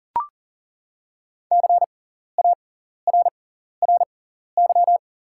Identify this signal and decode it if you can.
Morse code, heard as a keyed tone, about 40 wpm, spaced as for 14 wpm, 700 Hz. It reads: CARRY
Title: 6x Character Spacing - Top 300 Words - 40wpm